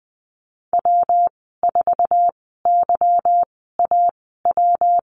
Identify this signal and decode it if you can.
Morse code, heard as a keyed tone, about 20 wpm, spaced as for 15 wpm, 700 Hz. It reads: W4YAW
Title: 179 Lesson 29g - Slash - Call Signs 20wpm